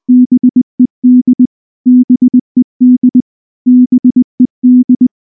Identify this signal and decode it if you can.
Morse code, heard as a keyed tone, about 20 wpm, 250 Hz. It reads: BED BED BED